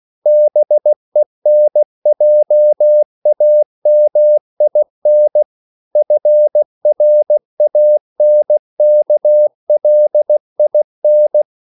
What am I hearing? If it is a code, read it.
Morse code, heard as a keyed tone, about 16 wpm, 600 Hz. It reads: BENJAMIN FRANKLIN
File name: CW / Practice / EnglishQuotations0416